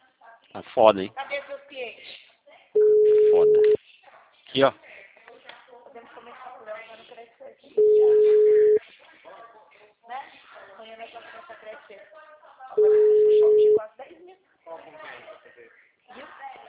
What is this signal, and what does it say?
Ringback tone (Europe, 425 Hz), with speech.